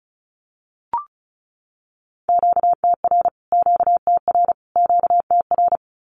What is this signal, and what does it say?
Morse code, heard as a keyed tone, about 34 wpm, 700 Hz. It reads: QTR QTR QTR
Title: Q-Codes Repeated 3x 35wpm